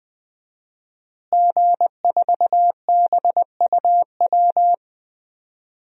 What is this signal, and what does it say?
Morse code, heard as a keyed tone, about 20 wpm, 700 Hz. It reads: G4BUW
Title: DXCC-100 Call Signs - 5 Characters - Repeated 3x - 20wpm